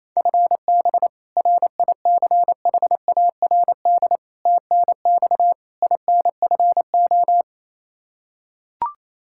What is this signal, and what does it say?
Morse code, heard as a keyed tone, about 28 wpm, 700 Hz. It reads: FB RICHARD TNX INFO
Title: Practice QSO - Exchange 3 - More Info - 28wpm